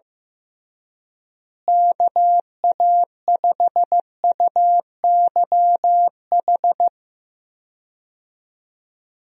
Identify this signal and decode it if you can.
Morse code, heard as a keyed tone, about 15 wpm, 700 Hz. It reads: KA5UYH
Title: USA Call Signs - 6 Characters - Repeated 3x - 15wpm